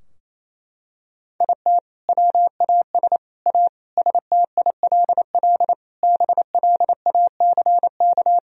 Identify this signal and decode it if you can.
Morse code, heard as a keyed tone, about 28 wpm, 700 Hz. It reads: IT WAS A STILL BLACK
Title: Sentences from Top 1000 Words - Review Part 2 - 28wpm